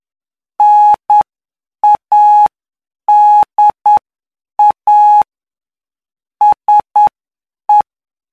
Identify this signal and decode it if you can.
Morse code, heard as a keyed tone, about 9 wpm, spaced as for 7 wpm, 840 Hz. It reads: NADA SE
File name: cw / py2lcd / 17-PY2LCD-Li��o 26